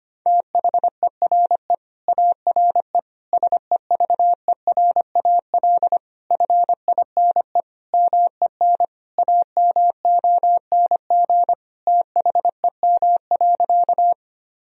Morse code, heard as a keyed tone, about 25 wpm, 700 Hz. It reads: THERE ARE SEVERAL FINE MEN AMONG THEM.
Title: Sentences from Top 1000 Words - Review Part 2 - 25wpm